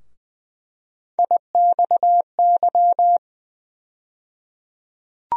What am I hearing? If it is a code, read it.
Morse code, heard as a keyed tone, about 20 wpm, 700 Hz. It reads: IXY
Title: USA Call Sign Postfixes - 3 Letters 20wpm